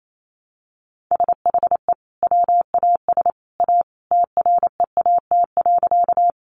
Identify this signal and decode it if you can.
Morse code, heard as a keyed tone, about 28 wpm, 700 Hz. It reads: SHE WAS A TREAT.
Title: Sentences from Top 1300 Words 28wpm